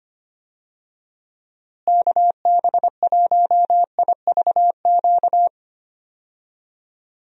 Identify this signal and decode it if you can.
Morse code, heard as a keyed tone, about 25 wpm, 700 Hz. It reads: KB1IVQ
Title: USA Call Signs - 6 Characters - Repeated 3x - 25wpm